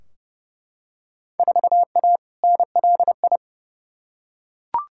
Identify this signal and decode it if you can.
Morse code, heard as a keyed tone, about 30 wpm, 700 Hz. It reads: 4A NLI
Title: Field Day Exchanges Encore - 30wpm